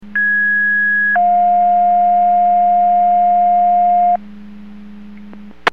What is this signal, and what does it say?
Two-tone page, heard as a one-second tone, then a three-second tone, about 1650 then 710 Hz. It is heard from 0.2 to 4.2 s.
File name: Portage / 00079 11-01 Portage 154.385 Paging